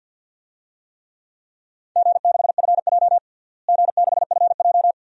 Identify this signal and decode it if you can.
Morse code, heard as a keyed tone, about 46 wpm, 700 Hz. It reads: G6FJ G6FJ
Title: DXCC-100 Call Signs - 4 Characters - Repeated 3x - 50wpm